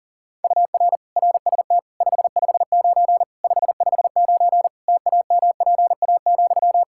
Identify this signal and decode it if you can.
Morse code, heard as a keyed tone, about 40 wpm, 700 Hz. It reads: UR RST 559 559 TAMPA,
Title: Practice QSO - Exchange 1 - Beginning - 40wpm